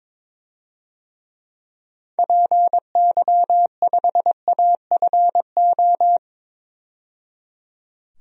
Morse code, heard as a keyed tone, about 22 wpm, 700 Hz. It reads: PY5AFO
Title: DXCC-100 Call Signs - 6 Characters - 22wpm